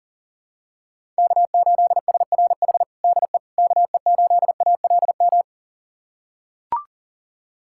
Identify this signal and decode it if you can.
Morse code, heard as a keyed tone, about 40 wpm, 700 Hz. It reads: K8SRH DE KE8ALM
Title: USA Callsign DE Callsign 40wpm